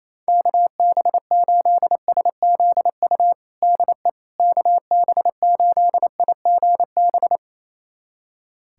Morse code, heard as a keyed tone, about 27 wpm, 700 Hz. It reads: KB8SZU DE KB8IGB
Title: USA Callsign DE Callsign - 6 Characters - 28wpm